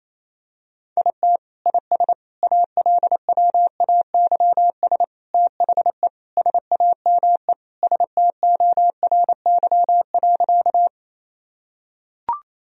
Morse code, heard as a keyed tone, about 28 wpm, 700 Hz. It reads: IT IS ALWAYS THE SAME STORY.
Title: Sentences from Top 1000 Words - Review Part 1 - 28wpm